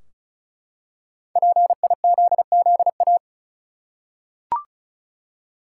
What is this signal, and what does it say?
Morse code, heard as a keyed tone, about 35 wpm, 700 Hz. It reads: PIZZA
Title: Five Letter Words - 35wpm